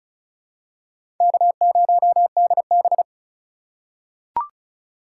Morse code, heard as a keyed tone, about 36 wpm, 700 Hz. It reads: K0DB
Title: USA Call Signs - 4 Characters - 2x Character Spacing - 35wpm